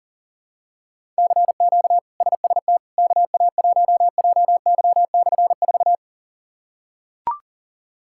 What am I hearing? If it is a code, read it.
Morse code, heard as a keyed tone, about 40 wpm, 700 Hz. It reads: CQ SST KA1JY/4
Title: K1USN SST QSO Practice - 40wpm